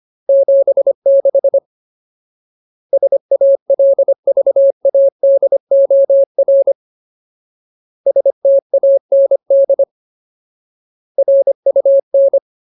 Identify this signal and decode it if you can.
Morse code, heard as a keyed tone, about 25 wpm, 550 Hz. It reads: 76 SALVADOR STAND RUN
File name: Ultimate Word List / 25WPM Ultimate